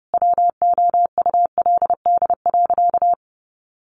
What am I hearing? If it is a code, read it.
Morse code, heard as a keyed tone, about 30 wpm, 700 Hz. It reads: WOULD.
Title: Sentences from Top 1500 Words 30wpm